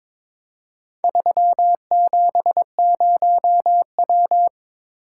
Morse code, heard as a keyed tone, about 21 wpm, 700 Hz. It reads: 370W
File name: practice / 22wpm / Four Letter-Number Combinations - 22wpm